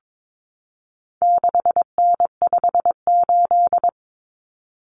Morse code, heard as a keyed tone, about 22 wpm, 700 Hz. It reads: 6N58
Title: Common Cut Numbers - 4-Digit 22wpm